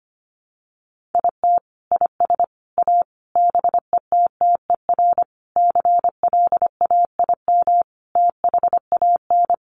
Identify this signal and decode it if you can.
Morse code, heard as a keyed tone, about 25 wpm, 700 Hz. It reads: IT IS A BETTER CLAIM THAN
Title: Sentences from Top 1500 Words 25wpm